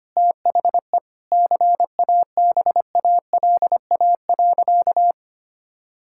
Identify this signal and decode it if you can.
Morse code, heard as a keyed tone, about 25 wpm, 700 Hz. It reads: THE CABALA.
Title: Pangram Challenge v2 - 25wpm